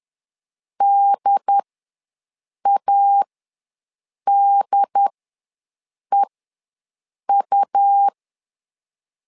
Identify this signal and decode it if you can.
Morse code, heard as a keyed tone, about 10 wpm, spaced as for 6 wpm, 800 Hz. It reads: DADEU
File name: LEZIONI / 2 UHDN / 2 UHD ws 10wpm 2058